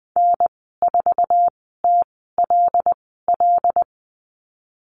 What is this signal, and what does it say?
Morse code, heard as a keyed tone, about 20 wpm, spaced as for 15 wpm, 700 Hz. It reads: N4TLL
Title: 122 Lesson 20g - Number 3 - Call Signs 20wpm